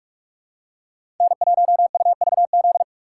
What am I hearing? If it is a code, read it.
Morse code, heard as a keyed tone, about 46 wpm, 700 Hz. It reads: N1UVZ
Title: USA Call Signs - 5 Characters - 6x Character Spacing - 45wpm